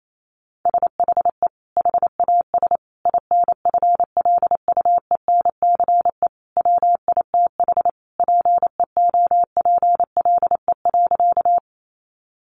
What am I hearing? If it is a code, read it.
Morse code, heard as a keyed tone, about 28 wpm, 700 Hz. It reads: SHE HAS INFLUENCE WITH PEOPLE.
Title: Sentences from Top 1100 Words 28wpm